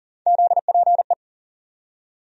Morse code, heard as a keyed tone, about 40 wpm, 700 Hz. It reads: ZPE
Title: USA Call Sign Postfixes - 3 Letters 40wpm